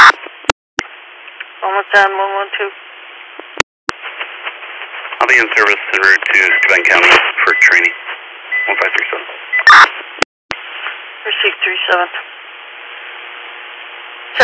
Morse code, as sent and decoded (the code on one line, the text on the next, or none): none